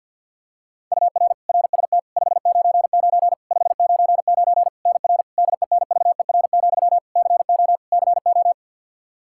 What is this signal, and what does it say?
Morse code, heard as a keyed tone, about 50 wpm, 700 Hz. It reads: UR RST 599 599 NR DENVER, CO CO
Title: Practice QSO - Exchange 1 - Beginning - 50wpm